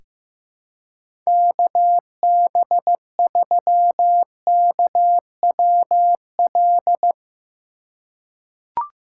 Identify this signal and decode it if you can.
Morse code, heard as a keyed tone, about 15 wpm, 700 Hz. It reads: KB3KWL
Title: USA Call Signs - 2x Character Spacing - 15wpm